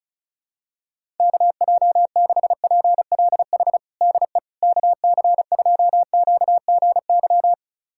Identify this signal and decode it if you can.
Morse code, heard as a keyed tone, about 36 wpm, 700 Hz. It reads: KJ6PLH DE KC2QGY
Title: USA Callsign DE Callsign - 6 Characters - 35wpm